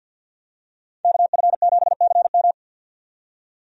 Morse code, heard as a keyed tone, about 52 wpm, 700 Hz. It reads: KF7CM
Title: USA Call Signs - 5 Characters - 50wpm